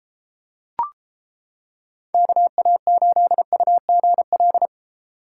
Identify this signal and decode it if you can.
Morse code, heard as a keyed tone, about 29 wpm, 700 Hz. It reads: KA8UGL